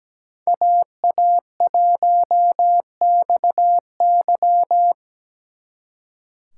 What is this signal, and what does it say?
Morse code, heard as a keyed tone, about 20 wpm, 700 Hz. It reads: AA1XY
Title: Speed Racing - USA 2x2 Call Signs 17wpm